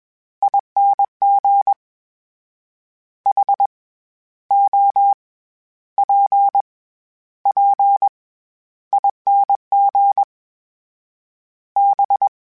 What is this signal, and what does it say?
Morse code, heard as a keyed tone, about 21 wpm, 800 Hz. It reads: ING H O P P ING B